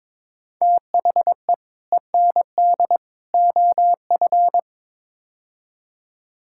Morse code, heard as a keyed tone, about 22 wpm, 700 Hz. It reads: THE END OF
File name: practice / 22wpm / Build A Phrase - 5-Word Phrases - 22wpm